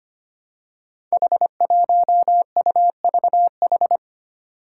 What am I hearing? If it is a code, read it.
Morse code, heard as a keyed tone, about 25 wpm, 700 Hz. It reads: H1UVH
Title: Five Challenging Letter-Number Combinations 25wpm